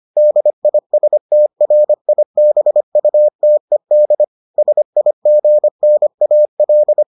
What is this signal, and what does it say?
Morse code, heard as a keyed tone, about 25 wpm, 600 Hz. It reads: DISTRIBUTED SIGNAL